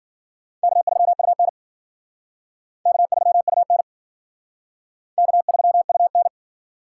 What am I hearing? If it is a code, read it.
Morse code, heard as a keyed tone, about 54 wpm, 700 Hz. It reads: K3UN K3UN K3UN